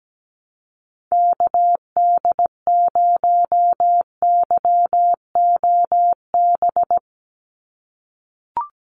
Morse code, heard as a keyed tone, about 17 wpm, 700 Hz. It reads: KD0YOB
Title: NA SOTA Activator Callsigns 17wpm